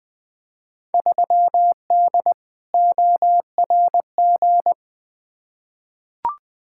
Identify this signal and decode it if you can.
Morse code, heard as a keyed tone, about 20 wpm, 700 Hz. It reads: 3D ORG
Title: Field Day Exchanges Encore - 20wpm